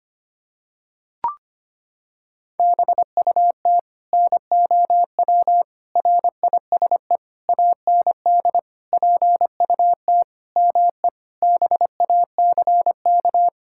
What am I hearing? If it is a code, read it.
Morse code, heard as a keyed tone, about 25 wpm, 700 Hz. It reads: BUT NOW RISE AND PUT ME BACK
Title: Sentences from Top 1000 Words - Review Part 6 - 25wpm